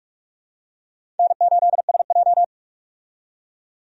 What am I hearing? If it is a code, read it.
Morse code, heard as a keyed tone, about 47 wpm, 700 Hz. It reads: N8SJ
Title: USA Call Signs - 4 Characters - 45wpm